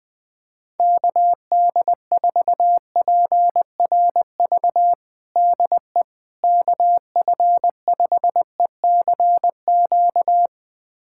Morse code, heard as a keyed tone, about 20 wpm, 700 Hz. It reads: KD4PRV DE KF5ECQ
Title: USA Callsign DE Callsign - 6 Characters - 20wpm